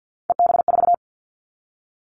Morse code, heard as a keyed tone, about 50 wpm, 700 Hz. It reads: EB4
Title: Callsign Prefixes for Top 25 DXCC - 50wpm